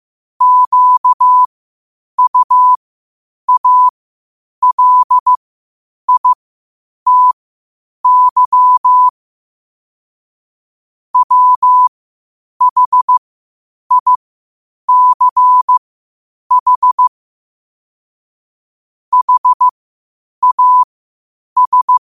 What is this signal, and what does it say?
Morse code, heard as a keyed tone, about 15 wpm, spaced as for 8 wpm, 1000 Hz. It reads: QUALITY WHICH HAS
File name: qotdfiles / 8wpm